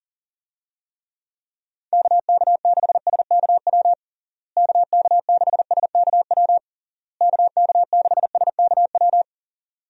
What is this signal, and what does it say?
Morse code, heard as a keyed tone, about 40 wpm, 700 Hz. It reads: KK6SKW KK6SKW KK6SKW
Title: USA Call Signs - 6 Characters - Repeated 3x - 40wpm